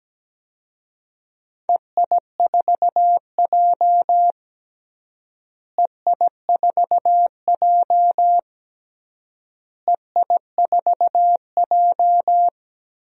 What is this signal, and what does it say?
Morse code, heard as a keyed tone, about 17 wpm, 700 Hz. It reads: EI4J EI4J EI4J
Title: DXCC-100 Call Signs - 4 Characters - Repeated 3x - 17wpm